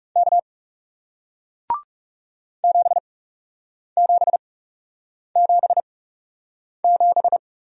Morse code, heard as a keyed tone, about 40 wpm, spaced as for 8 wpm, 700 Hz. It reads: K 7777